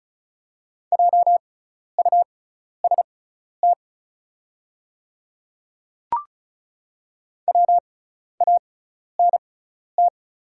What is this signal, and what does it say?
Morse code, heard as a keyed tone, about 36 wpm, spaced as for 12 wpm, 690 Hz. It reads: JUST WANT
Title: 6x Character Spacing - Top 200 Words - 35wpm